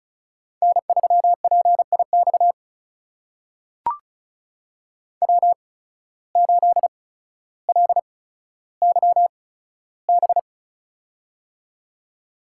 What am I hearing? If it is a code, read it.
Morse code, heard as a keyed tone, about 36 wpm, spaced as for 15 wpm, 700 Hz. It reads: N3PIX W8LYB
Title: USA Call Signs - 5 Characters - 8x Character Spacing - 35wpm